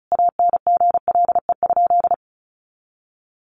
Morse code, heard as a keyed tone, about 34 wpm, 700 Hz. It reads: ANGLE?
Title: Sentences from Top 1500 Words 35wpm